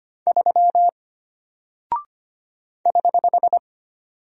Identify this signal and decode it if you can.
Morse code, heard as a keyed tone, about 25 wpm, 700 Hz. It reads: 3 <HH>
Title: Single Character (Letter:Number:Punctuation:Prosign) Alt 25wpm